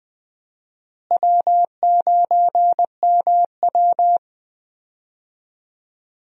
Morse code, heard as a keyed tone, about 20 wpm, 700 Hz. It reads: W9MW
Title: USA Call Signs - 4 Characters - Repeated 2x - 20wpm